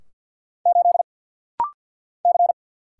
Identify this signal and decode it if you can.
Morse code, heard as a keyed tone, about 48 wpm, 700 Hz. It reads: 8 C
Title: Single Letter-Number - Rapid-Fire 50wpm